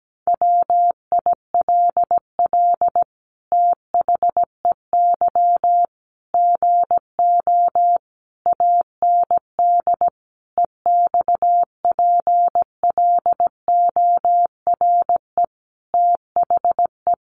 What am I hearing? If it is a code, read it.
Morse code, heard as a keyed tone, about 17 wpm, 700 Hz. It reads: WILL THEY GO AND EXPLORE THE
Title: Sentences from Top 1100 Words 17wpm